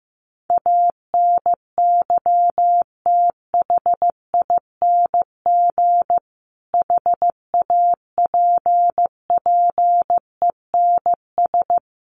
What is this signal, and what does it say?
Morse code, heard as a keyed tone, about 15 wpm, 700 Hz. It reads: ANYTHING HAPPENS
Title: A Sound Of Thunder 15wpm